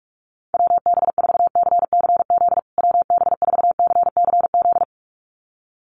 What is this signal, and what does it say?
Morse code, heard as a keyed tone, about 44 wpm, 700 Hz. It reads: WB4CCZ WB4CCZ
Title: USA Call Signs Repeated 2x 45wpm